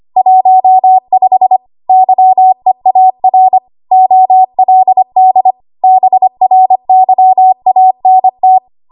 Morse code, heard as a keyed tone, about 25 wpm, 750 Hz. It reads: 15 YEAR OLD BRYANT
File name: Archive / 25 WPM / 25WPM